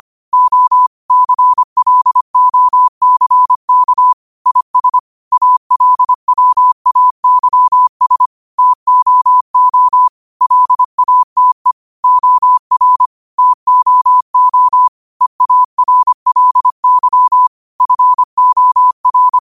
Morse code, heard as a keyed tone, about 25 wpm, 1000 Hz. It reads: O CLOCK IS ALWAYS TOO LATE OR TOO EARLY FOR
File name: qotdfiles / 25wpm